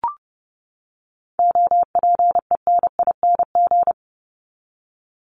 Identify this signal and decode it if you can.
Morse code, heard as a keyed tone, about 30 wpm, 700 Hz. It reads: OPENING
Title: Top 1751-2000 Words 30wpm